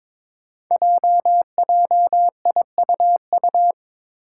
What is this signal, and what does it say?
Morse code, heard as a keyed tone, about 22 wpm, 700 Hz. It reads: JJIUU
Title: Five Challenging Letter-Number Combinations 22wpm